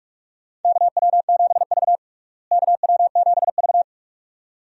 Morse code, heard as a keyed tone, about 46 wpm, 700 Hz. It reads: KW7V KW7V
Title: USA Call Signs - 4 Characters - Repeated 2x - 45wpm